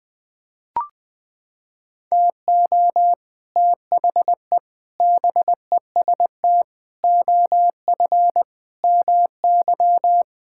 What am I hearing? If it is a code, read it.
Morse code, heard as a keyed tone, about 20 wpm, 700 Hz. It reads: TO THE BEST OF MY
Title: Sentences with 6-Word Phrases - 20wpm